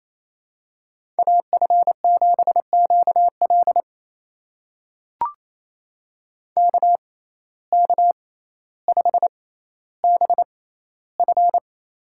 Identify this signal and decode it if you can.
Morse code, heard as a keyed tone, about 27 wpm, spaced as for 14 wpm, 700 Hz. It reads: AF7QL KK5BF
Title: USA Call Signs - 5 Characters - 6x Character Spacing - 28wpm